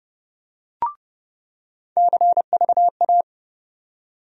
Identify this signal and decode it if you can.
Morse code, heard as a keyed tone, about 30 wpm, 700 Hz. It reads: CVA
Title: USA Call Sign Postfixes - 3 Letters 30wpm